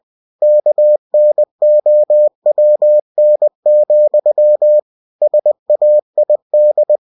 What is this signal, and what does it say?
Morse code, heard as a keyed tone, about 20 wpm, 600 Hz. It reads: KNOWN, SAID
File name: cw-books / winnie-the-pooh / winnie-20wpm-page-0108